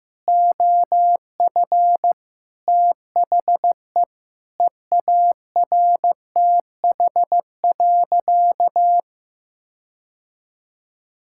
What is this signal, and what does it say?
Morse code, heard as a keyed tone, about 15 wpm, 700 Hz. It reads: OF THE EARTH.
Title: Sentences from Top 1600 Words 15wpm